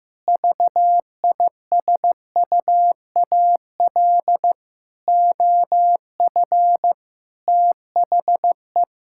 Morse code, heard as a keyed tone, about 15 wpm, 700 Hz. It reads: VISUAL OF THE
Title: Sentences from Top 1000 Words - Review Part 6 - 15wpm